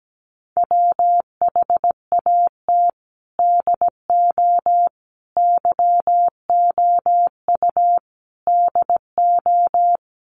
Morse code, heard as a keyed tone, about 17 wpm, 700 Hz. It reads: WHAT DO YOU DO
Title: Sets of 4 Words 17wpm